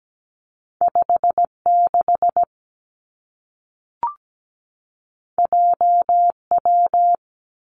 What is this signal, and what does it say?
Morse code, heard as a keyed tone, about 17 wpm, 700 Hz. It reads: 56 JW
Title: Two Challenging Letter-Number Combinations 17wpm